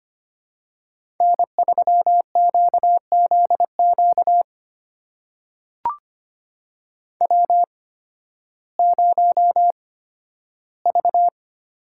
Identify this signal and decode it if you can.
Morse code, heard as a keyed tone, about 25 wpm, spaced as for 13 wpm, 700 Hz. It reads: N3QZQ W0V
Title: USA Call Signs - 5 Characters - 8x Character Spacing - 25wpm